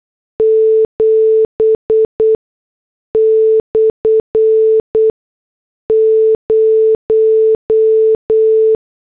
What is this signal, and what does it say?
Morse code, heard as a keyed tone, about 8 wpm, 440 Hz. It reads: 7 / 0
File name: morse / headlines